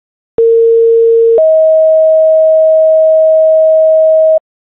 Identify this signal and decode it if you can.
Two-tone page, heard as a one-second tone, then a three-second tone, about 460 then 640 Hz, with noise. It is heard from 0.4 to 4.4 s.